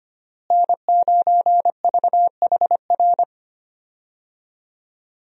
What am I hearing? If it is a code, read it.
Morse code, heard as a keyed tone, about 25 wpm, 700 Hz. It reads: N9VHR
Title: USA Call Signs - 5 Characters - Repeated 3x - 25wpm